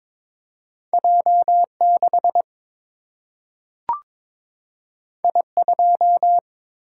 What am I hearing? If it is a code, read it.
Morse code, heard as a keyed tone, about 22 wpm, 700 Hz. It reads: J 6 I 2
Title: Single-Letter Single-Number 22wpm